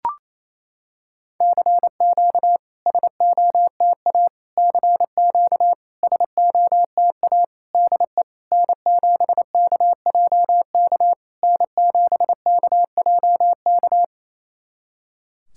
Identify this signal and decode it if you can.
Morse code, heard as a keyed tone, about 28 wpm, 690 Hz. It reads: CQ SOTA CQ SOTA DE N7KJK N7KJK
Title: SOTA QSO Practice 28wpm